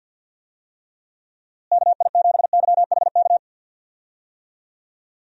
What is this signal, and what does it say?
Morse code, heard as a keyed tone, about 51 wpm, 700 Hz. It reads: KI7YHK
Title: USA Call Signs - 6 Characters - Repeated 3x - 50wpm